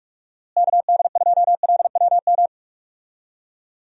Morse code, heard as a keyed tone, about 46 wpm, 700 Hz. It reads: KD2LWM
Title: USA Call Signs - 6 Characters - Repeated 3x - 45wpm